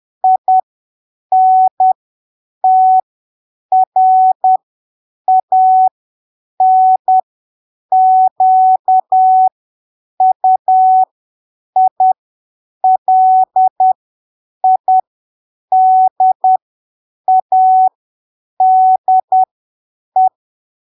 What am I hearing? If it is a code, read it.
Morse code, heard as a keyed tone, about 10 wpm, spaced as for 7 wpm, 750 Hz. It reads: INTRANQUILIDADE
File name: arquivos / texto / texto 05 10ppm